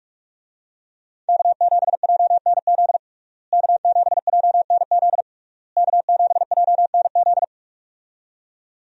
Morse code, heard as a keyed tone, about 41 wpm, 700 Hz. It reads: K7JNZ K7JNZ K7JNZ
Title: USA Call Signs - 5 Characters - Repeated 3x - 45wpm